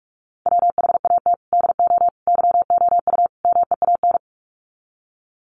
Morse code, heard as a keyed tone, about 44 wpm, 700 Hz. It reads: WHAT DO YOU MEAN
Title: Sets of 4 Words 45wpm